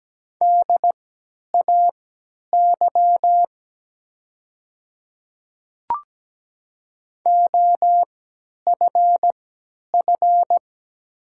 Morse code, heard as a keyed tone, about 17 wpm, spaced as for 10 wpm, 700 Hz. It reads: DAY OFF
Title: 3x Character Spacing - Top 200 Words - 17wpm